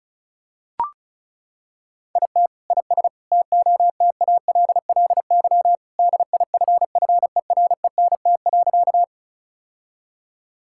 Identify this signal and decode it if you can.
Morse code, heard as a keyed tone, about 34 wpm, 700 Hz. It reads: IT IS TOTALLY DIFFERENT.
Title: Sentences from Top 1600 Words 35wpm